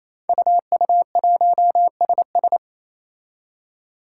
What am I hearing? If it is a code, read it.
Morse code, heard as a keyed tone, about 27 wpm, 700 Hz. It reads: UU1SS
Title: Five Challenging Letter-Number Combinations 28wpm